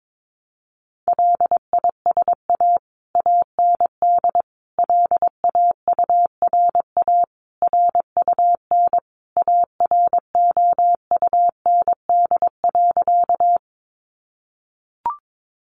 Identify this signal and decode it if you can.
Morse code, heard as a keyed tone, about 22 wpm, 700 Hz. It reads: LISA AND LAURA RUN AROUND.
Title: Additional Easy Phrases 22wpm